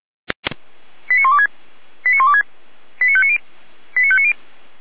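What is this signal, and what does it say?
Five-tone selective call (ZVEI), heard as quick runs of five tones, tones of about 70 ms. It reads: 89126 89126 89590 89590
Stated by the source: ZVEI five-tone selective call 89208 received on an SDR.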